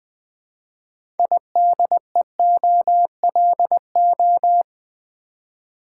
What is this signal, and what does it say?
Morse code, heard as a keyed tone, about 20 wpm, 700 Hz. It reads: IDEOLO
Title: Build A Word - 3 Letters At A Time - 8-Letter Words - 20wpm